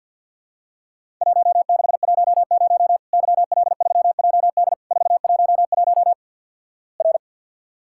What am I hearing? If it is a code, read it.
Morse code, heard as a keyed tone, about 48 wpm, 700 Hz. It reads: 1610 YL3JD 411 R
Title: Worked All Europe DX Contest - QTC Exchange with Repeats - 50wpm